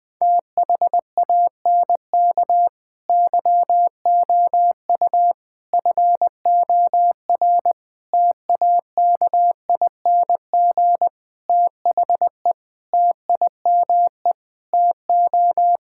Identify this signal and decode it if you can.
Morse code, heard as a keyed tone, about 20 wpm, 700 Hz. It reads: THANK YOU FOR TAKING THE TIME TO
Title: Sets of 9 Words - 20wpm